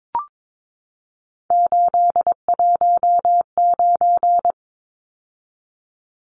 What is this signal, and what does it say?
Morse code, heard as a keyed tone, about 22 wpm, 700 Hz. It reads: 819
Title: Numbers - 3-Digit 22wpm